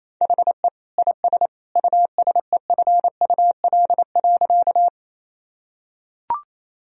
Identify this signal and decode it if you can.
Morse code, heard as a keyed tone, about 27 wpm, 700 Hz. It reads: HE IS USEFUL.
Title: Sentences from Top 1000 Words - Review Part 4 - 28wpm